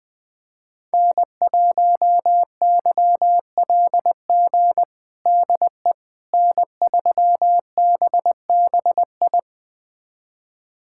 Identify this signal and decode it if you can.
Morse code, heard as a keyed tone, about 21 wpm, 700 Hz. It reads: N1YLG DE N3BBI
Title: USA Callsign DE Callsign - 5 Characters - 20wpm